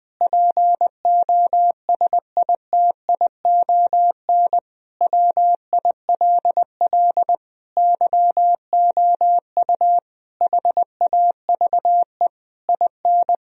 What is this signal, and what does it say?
Morse code, heard as a keyed tone, about 20 wpm, 700 Hz. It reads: POSITION WILL YOU HAVE IN
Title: Sentences from Top 1000 Words - Review Part 3 - 20wpm